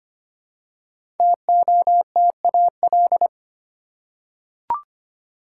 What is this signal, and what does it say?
Morse code, heard as a keyed tone, about 25 wpm, 700 Hz. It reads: TOTAL
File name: 5 Letter Words - First 3 Letters Given - 25wpm